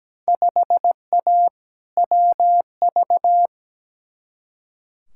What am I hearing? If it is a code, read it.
Morse code, heard as a keyed tone, about 17 wpm, 700 Hz. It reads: 5A WV
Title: Field Day Exchanges - 17wpm